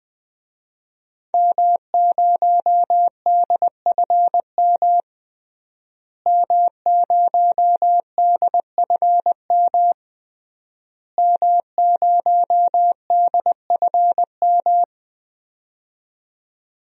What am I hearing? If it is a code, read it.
Morse code, heard as a keyed tone, about 21 wpm, 700 Hz. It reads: M0DFM M0DFM M0DFM
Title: DXCC-100 Call Signs - 5 Characters - Repeated 3x - 20wpm